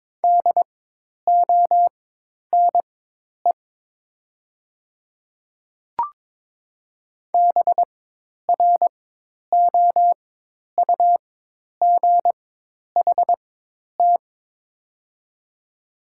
Morse code, heard as a keyed tone, about 22 wpm, spaced as for 10 wpm, 700 Hz. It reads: DONE BROUGHT
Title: 4x Character Spacing - Top 500 Words - 22wpm